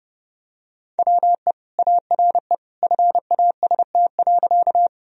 Morse code, heard as a keyed tone, about 30 wpm, 700 Hz. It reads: WE ARE FAST.
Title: Sentences from Top 1000 Words - Review Part 2 - 30wpm